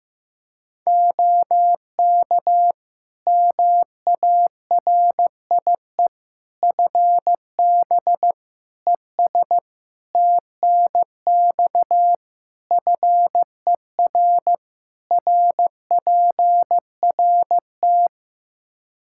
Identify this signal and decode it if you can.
Morse code, heard as a keyed tone, about 16 wpm, 700 Hz. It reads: OK MARIE FB ES TNX FER RPRT
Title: Practice QSO - Exchange 2 - Info - 15wpm